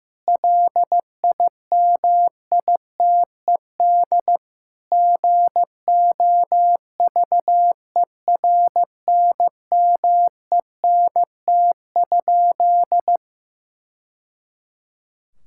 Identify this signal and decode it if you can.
Morse code, heard as a keyed tone, about 15 wpm, 700 Hz. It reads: LIMITED GOVERNMENT?
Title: Sentences from Top 1000 Words - Review Part 6 - 15wpm